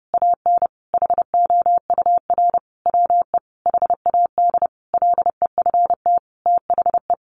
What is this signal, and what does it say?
Morse code, heard as a keyed tone, about 30 wpm, 700 Hz. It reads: AN HOUR WE HAD LEFT THE
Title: Sentences from Top 1300 Words 30wpm